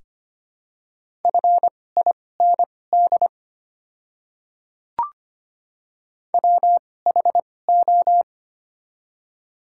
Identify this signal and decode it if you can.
Morse code, heard as a keyed tone, about 25 wpm, spaced as for 18 wpm, 700 Hz. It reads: FIND WHO
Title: 2x Character Spacing - Top 300 Words - 25wpm